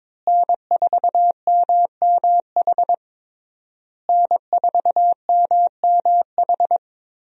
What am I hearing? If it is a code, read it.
Morse code, heard as a keyed tone, about 22 wpm, 700 Hz. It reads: N4MMH N4MMH
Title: USA Call Signs - 5 Characters - Repeated 3x - 22wpm